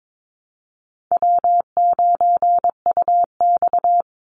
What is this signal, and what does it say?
Morse code, heard as a keyed tone, about 22 wpm, 700 Hz. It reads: W9UX
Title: NA SOTA Chaser Callsigns 22wpm